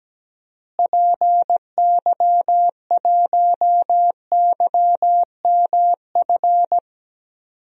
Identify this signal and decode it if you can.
Morse code, heard as a keyed tone, about 17 wpm, 700 Hz. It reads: PY1YMF
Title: DXCC-100 Call Signs - 6 Characters - Repeated 2x - 17wpm